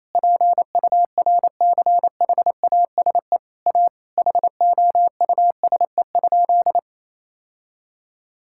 Morse code, heard as a keyed tone, about 28 wpm, 700 Hz. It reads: PURCHASE A HOUSE?
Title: Sentences from Top 1000 Words - Review Part 6 - 28wpm